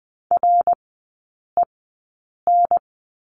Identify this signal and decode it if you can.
Morse code, heard as a keyed tone, about 20 wpm, spaced as for 8 wpm, 690 Hz. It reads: REN
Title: 087 Lesson 15e - Number 5 - Three Characters 20wpm